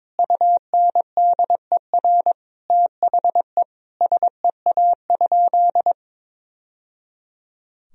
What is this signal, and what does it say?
Morse code, heard as a keyed tone, about 22 wpm, 700 Hz. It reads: UNDER THE SEA?
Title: Sentences from Top 1000 Words - Review Part 1 - 22wpm